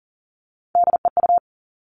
Morse code, heard as a keyed tone, about 42 wpm, 700 Hz. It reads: DEU
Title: Three Challenging Letter-Number Combinations 40wpm